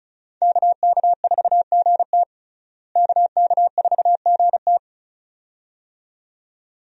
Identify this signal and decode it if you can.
Morse code, heard as a keyed tone, about 34 wpm, 700 Hz. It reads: KK4GT KK4GT
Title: USA Call Signs - 5 Characters - Repeated 2x - 35wpm